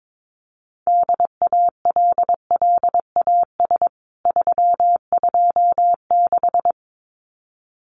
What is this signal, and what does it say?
Morse code, heard as a keyed tone, about 22 wpm, 700 Hz. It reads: DALLAS 326
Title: CWT Exchange 22wpm